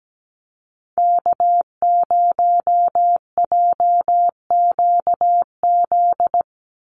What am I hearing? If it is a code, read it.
Morse code, heard as a keyed tone, about 17 wpm, 700 Hz. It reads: K0JQZ
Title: NA SOTA Activator Callsigns 17wpm